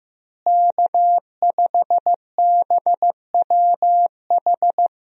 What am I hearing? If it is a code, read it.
Morse code, heard as a keyed tone, about 15 wpm, 700 Hz. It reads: K5BWH
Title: USA Call Signs - 5 Characters - Repeated 2x - 15wpm